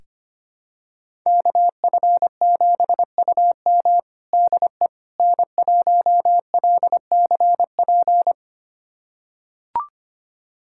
Morse code, heard as a keyed tone, about 25 wpm, 700 Hz. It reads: KF7UM DE N1LCP
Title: USA Callsign DE Callsign - 5 Characters - 25wpm